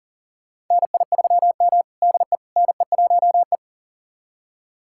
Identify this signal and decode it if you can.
Morse code, heard as a keyed tone, about 40 wpm, 700 Hz. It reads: NI3M DE NE1E
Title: USA Callsign DE Callsign - 4 Characters - 40wpm